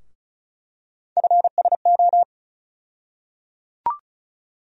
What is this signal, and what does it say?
Morse code, heard as a keyed tone, about 36 wpm, 700 Hz. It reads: FSO
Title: USA Call Sign Postfixes - 3 Letters 35wpm